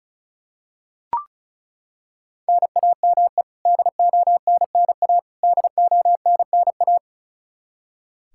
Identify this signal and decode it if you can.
Morse code, heard as a keyed tone, about 36 wpm, 700 Hz. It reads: NAME DONNA DONNA
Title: Practice QSO - Exchange 1 - Beginning - 35wpm